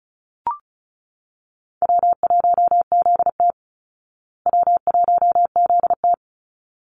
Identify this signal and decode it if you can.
Morse code, heard as a keyed tone, about 34 wpm, 700 Hz. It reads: W1ZT W1ZT
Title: Easy Call Signs Repeated 2x - 35wpm